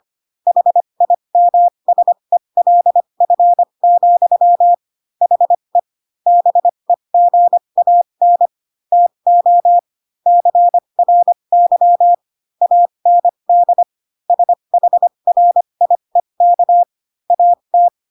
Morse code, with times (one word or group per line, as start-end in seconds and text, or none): 0.5-4.7 HIMSELF,
5.2-5.8 HE
6.3-8.5 BEGAN
8.9-9.8 TO
10.3-12.1 CRY
12.6-13.8 AND
14.3-16.8 SHRIEK
17.3-17.9 AT